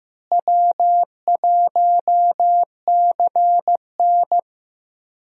Morse code, heard as a keyed tone, about 15 wpm, 700 Hz. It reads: W1CN
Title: USA Call Signs - 4 Characters - Repeated 3x - 15wpm